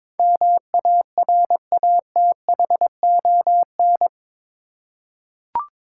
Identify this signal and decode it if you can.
Morse code, heard as a keyed tone, about 22 wpm, 700 Hz. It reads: MARATHON
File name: practice / 22wpm / Eight Letter Words - 22wpm